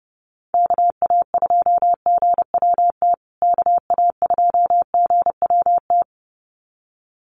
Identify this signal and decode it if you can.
Morse code, heard as a keyed tone, about 30 wpm, 700 Hz. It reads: KA2GWT KA2GWT
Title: USA Call Signs Repeated 2x 30wpm